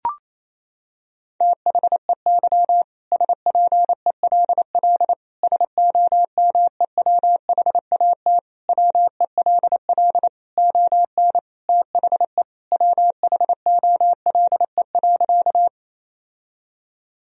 Morse code, heard as a keyed tone, about 28 wpm, 700 Hz. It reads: THEY SPELL SOMEWHAT WELL ON THE WHOLE.
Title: Sentences from Top 1000 Words - Review Part 1 - 28wpm